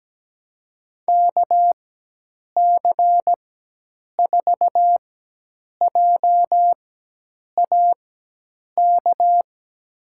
Morse code, heard as a keyed tone, about 17 wpm, spaced as for 8 wpm, 700 Hz. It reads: KC4JAK
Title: USA Call Signs - 4x Character Spacing - 17wpm